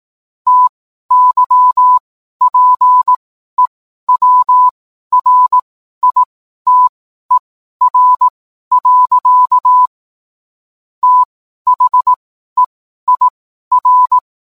Morse code, heard as a keyed tone, about 18 wpm, spaced as for 12 wpm, 1000 Hz. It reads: TYPEWRITER. THEIR
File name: cwblog / cw mp3 directory / mem morse 12